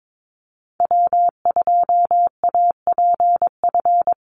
Morse code, heard as a keyed tone, about 22 wpm, 700 Hz. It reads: W2APF
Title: Call Signs Repeated 2x - 22wpm